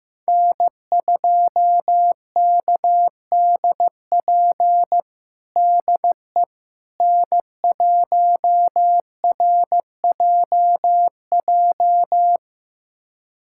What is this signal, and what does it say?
Morse code, heard as a keyed tone, about 15 wpm, 700 Hz. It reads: N2KDP DE N1RJJ
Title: USA Callsign DE Callsign - 5 Characters - 15wpm